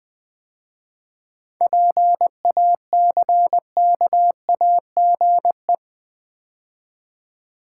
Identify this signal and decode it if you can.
Morse code, heard as a keyed tone, about 20 wpm, 700 Hz. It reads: PACKAGE
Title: 7 Letter Words - First 3 Letters Given - 20wpm